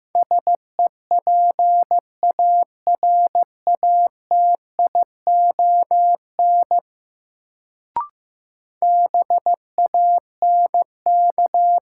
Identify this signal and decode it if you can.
Morse code, heard as a keyed tone, about 16 wpm, 700 Hz. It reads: SEPARATION BANK
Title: Sets of 5 Words - New Word Drill - 15wpm